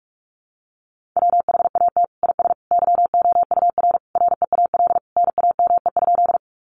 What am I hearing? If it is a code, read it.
Morse code, heard as a keyed tone, about 44 wpm, 700 Hz. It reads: WHAT IS YOUR REAL NAME?
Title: Sentences from Top 300 Words 45wpm